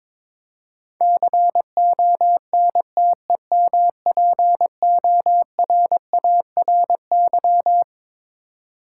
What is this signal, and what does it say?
Morse code, heard as a keyed tone, about 22 wpm, 700 Hz. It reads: CONTEMPORARY
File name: Twelve Letter Words - 22wpm